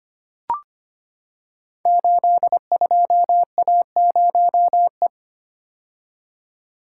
Morse code, heard as a keyed tone, about 25 wpm, 700 Hz. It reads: 82A0E
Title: Code Groups - 5 Letter:Numbers - 25wpm